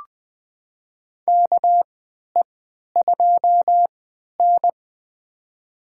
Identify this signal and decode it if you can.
Morse code, heard as a keyed tone, about 20 wpm, spaced as for 11 wpm, 700 Hz. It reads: KE2N
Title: USA Call Signs - 4 Characters - 3x Character Spacing - 20wpm